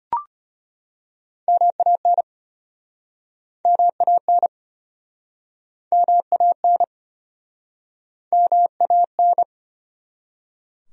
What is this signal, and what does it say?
Morse code, heard as a keyed tone, about 36 wpm, 700 Hz. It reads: MAN MAN MAN MAN